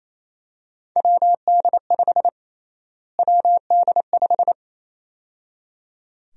Morse code, heard as a keyed tone, about 28 wpm, 700 Hz. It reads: WD5 WD5
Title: USA Call Sign Prefixes - Repeated 2x - 28wpm